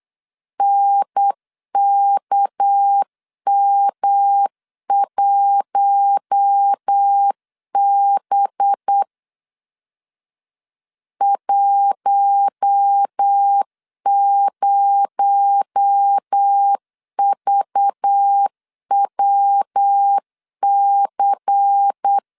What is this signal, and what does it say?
Morse code, heard as a keyed tone, about 8 wpm, 800 Hz. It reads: NKM1B 10VWC
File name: CW / DET / 1 6 1500C S B 8wpm 5142